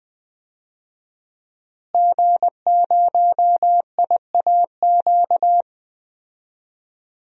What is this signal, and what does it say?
Morse code, heard as a keyed tone, about 20 wpm, 700 Hz. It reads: G0IAQ
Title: DXCC-100 Call Signs - 5 Characters - Repeated 3x - 20wpm